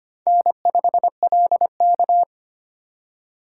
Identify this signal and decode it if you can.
Morse code, heard as a keyed tone, about 32 wpm, 700 Hz. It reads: N5LK